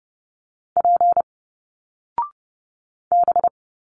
Morse code, heard as a keyed tone, about 30 wpm, 700 Hz. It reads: P B
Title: Single Letters 30wpm